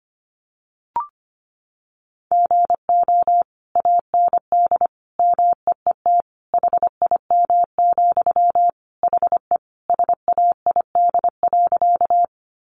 Morse code, heard as a keyed tone, about 25 wpm, 700 Hz. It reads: GO AND MEET HIM, HE SAID.
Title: Sentences from Top 1100 Words 25wpm